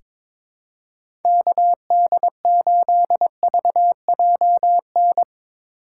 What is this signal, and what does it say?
Morse code, heard as a keyed tone, about 22 wpm, 700 Hz. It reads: KD8VJN
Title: USA Call Signs - 3x Character Spacing - 22wpm